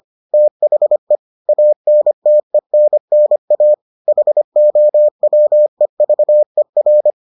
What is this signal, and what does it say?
Morse code, heard as a keyed tone, about 25 wpm, 600 Hz. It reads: THE ANTENNA HOWEVER